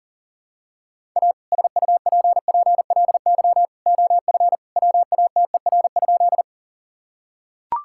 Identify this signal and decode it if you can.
Morse code, heard as a keyed tone, about 37 wpm, 700 Hz. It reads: A SUPPLY OF WATER?
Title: Sentences from Top 1000 Words - Review Part 5 - 40wpm